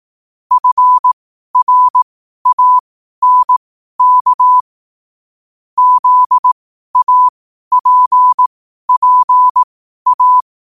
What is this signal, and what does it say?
Morse code, heard as a keyed tone, about 18 wpm, spaced as for 12 wpm, 1000 Hz. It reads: FRANK ZAPPA
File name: qotdfiles / 12wpm